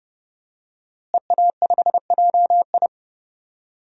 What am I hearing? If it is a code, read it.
Morse code, heard as a keyed tone, about 32 wpm, 700 Hz. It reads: EA5JI
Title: DXCC-100 Call Signs - 5 Characters - Repeated 3x - 30wpm